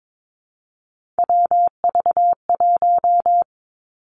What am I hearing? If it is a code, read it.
Morse code, heard as a keyed tone, about 22 wpm, 700 Hz. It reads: WV1
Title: Three Challenging Letter-Number Combinations 22wpm